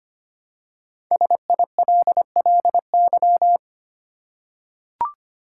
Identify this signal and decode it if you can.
Morse code, heard as a keyed tone, about 25 wpm, 700 Hz. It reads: SILLY
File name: 5 Letter Words - First 3 Letters Given - 25wpm